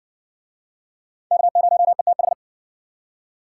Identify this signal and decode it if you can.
Morse code, heard as a keyed tone, about 58 wpm, 700 Hz. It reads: K0ETH